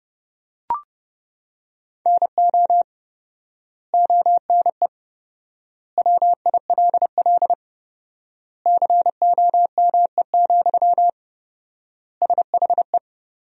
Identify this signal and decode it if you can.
Morse code, heard as a keyed tone, about 32 wpm, 700 Hz. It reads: NO ONE WILL COME, SHE
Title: 4x Word-Spacing - Sentences from Top 100 Words - 30wpm